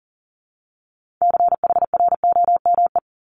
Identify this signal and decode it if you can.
Morse code, heard as a keyed tone, about 42 wpm, 700 Hz. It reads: CHROME
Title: Multi-letter Phonemes 40wpm